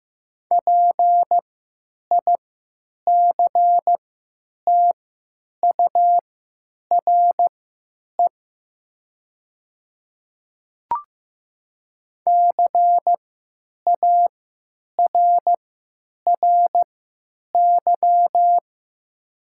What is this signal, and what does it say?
Morse code, heard as a keyed tone, about 15 wpm, spaced as for 8 wpm, 700 Hz. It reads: PICTURE CARRY
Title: 3x Character Spacing - Top 300 Words - 15wpm